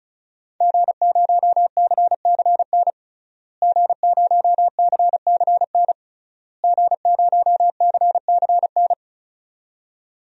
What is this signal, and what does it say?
Morse code, heard as a keyed tone, about 36 wpm, 700 Hz. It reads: G0CCN G0CCN G0CCN
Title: DXCC-100 Call Signs - 5 Characters - Repeated 3x - 35wpm